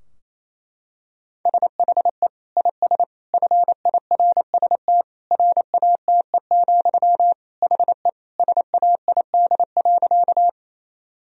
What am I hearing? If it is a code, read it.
Morse code, heard as a keyed tone, about 28 wpm, 700 Hz. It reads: SHE IS FIRST RATE, HE SAID.
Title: Sentences from Top 1000 Words - Review Part 4 - 28wpm